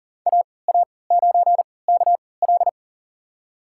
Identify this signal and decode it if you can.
Morse code, heard as a keyed tone, about 40 wpm, spaced as for 23 wpm, 700 Hz. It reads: AA9KL
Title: USA Call Signs - 5 Characters - 3x Character Spacing - 40wpm